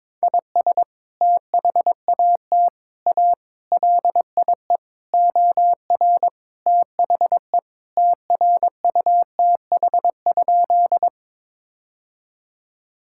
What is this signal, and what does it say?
Morse code, heard as a keyed tone, about 22 wpm, 700 Hz. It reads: IS THAT A LIE OR THE TRUTH?
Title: Sentences from Top 1600 Words 22wpm